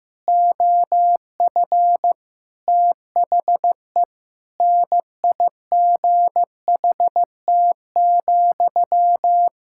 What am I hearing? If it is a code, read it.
Morse code, heard as a keyed tone, about 15 wpm, 700 Hz. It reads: OF THE NIGHT,
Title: Sentences from Top 1000 Words - Review Part 6 - 15wpm